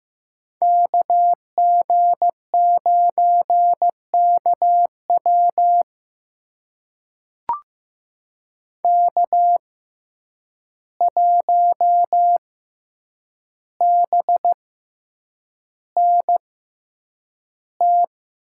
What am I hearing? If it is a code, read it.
Morse code, heard as a keyed tone, about 15 wpm, spaced as for 8 wpm, 700 Hz. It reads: KG9KW K1BNT
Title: USA Call Signs - 5 Characters - 6x Character Spacing - 15wpm